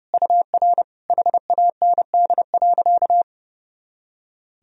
Morse code, heard as a keyed tone, about 30 wpm, 700 Hz. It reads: UR HAND.
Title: Sentences from Top 400 Words and Abbreviations 30wpm